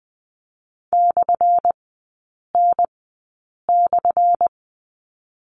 Morse code, heard as a keyed tone, about 20 wpm, spaced as for 8 wpm, 700 Hz. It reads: /N/
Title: 178 Lesson 29e - Slash - Three Characters 20wpm